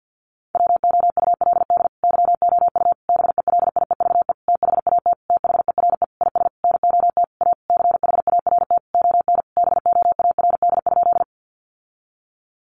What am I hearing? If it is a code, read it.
Morse code, heard as a keyed tone, about 50 wpm, 700 Hz. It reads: WOULD YOU BELIEVE THAT THERE IS NOT A CHART ON BOARD?
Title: Sentences from Top 1300 Words 50wpm